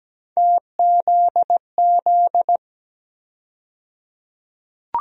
Morse code, heard as a keyed tone, about 17 wpm, 700 Hz. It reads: TZZ
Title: USA Call Sign Postfixes - 3 Letters 17wpm